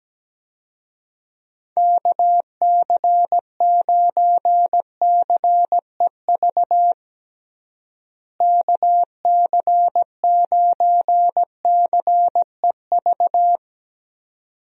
Morse code, heard as a keyed tone, about 17 wpm, 700 Hz. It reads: KC9CEV KC9CEV
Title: USA Call Signs - 6 Characters - Repeated 2x - 17wpm